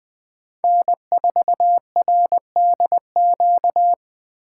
Morse code, heard as a keyed tone, about 23 wpm, 700 Hz. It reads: N4RDQ
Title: Speed Racing - USA Call Signs 20wpm